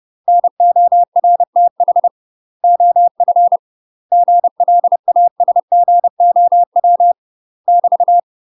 Morse code, heard as a keyed tone, about 30 wpm, 700 Hz. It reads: NORTH OF GLASGOW <BT>
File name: UK MASTER QSOs TRAINING / 30WPM / UK MASTER QSOs 0035